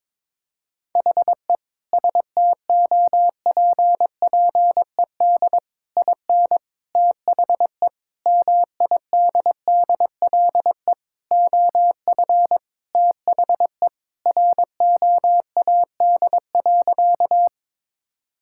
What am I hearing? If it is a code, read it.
Morse code, heard as a keyed tone, about 22 wpm, 700 Hz. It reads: HE STOPPED IN THE MIDDLE OF THE ROAD.
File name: practice / 22wpm / Sentences with 3-Word Phrases - 22wpm